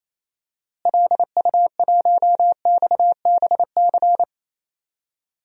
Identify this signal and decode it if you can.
Morse code, heard as a keyed tone, about 27 wpm, 700 Hz. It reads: LU1XBC
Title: DXCC-100 Call Signs - 6 Characters - Repeated 3x - 28wpm